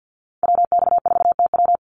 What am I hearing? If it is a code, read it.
Morse code, heard as a keyed tone, about 48 wpm, 700 Hz. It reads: WX4TW
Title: NA SOTA Activator Callsigns 50wpm